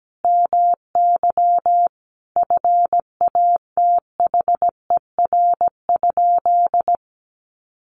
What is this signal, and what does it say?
Morse code, heard as a keyed tone, about 17 wpm, 700 Hz. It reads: MY FATHER?
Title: Sentences from Top 1400 Words 17wpm